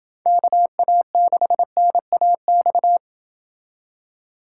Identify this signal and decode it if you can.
Morse code, heard as a keyed tone, about 28 wpm, 700 Hz. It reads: KA6NAX